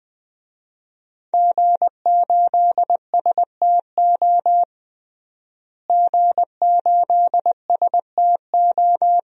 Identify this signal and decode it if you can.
Morse code, heard as a keyed tone, about 21 wpm, 700 Hz. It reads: G8STO G8STO
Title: DXCC-100 Call Signs - 5 Characters - Repeated 3x - 20wpm